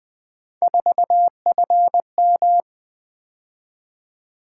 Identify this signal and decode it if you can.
Morse code, heard as a keyed tone, about 20 wpm, 700 Hz. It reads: 4FM
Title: Three Letter-Number Combinations - 20wpm